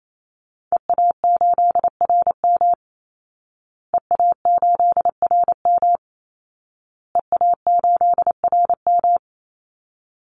Morse code, heard as a keyed tone, about 28 wpm, 700 Hz. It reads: EA8RM EA8RM EA8RM
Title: Call Signs Repeated 2x - 28wpm